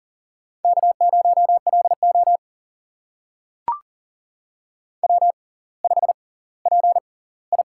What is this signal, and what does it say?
Morse code, heard as a keyed tone, about 39 wpm, spaced as for 20 wpm, 700 Hz. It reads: K0LO W5PI
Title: USA Call Signs - 4 Characters - 6x Character Spacing - 40wpm